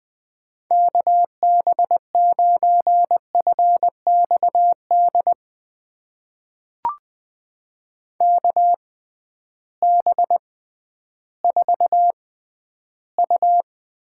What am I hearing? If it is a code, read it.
Morse code, heard as a keyed tone, about 20 wpm, spaced as for 12 wpm, 700 Hz. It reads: KB9FXD KB4U
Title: USA Call Signs - 6 Characters - 6x Character Spacing - 20wpm